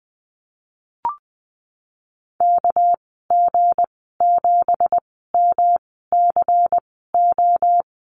Morse code, heard as a keyed tone, about 20 wpm, spaced as for 15 wpm, 700 Hz. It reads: KG7MCO
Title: 200 Lesson 32g - Letter k - Call Signs 20wpm